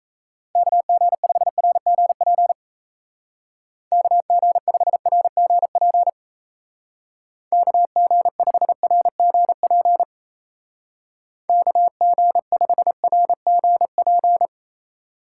Morse code, heard as a keyed tone, about 37 wpm, 700 Hz. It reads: KG5RGP KG5RGP KG5RGP KG5RGP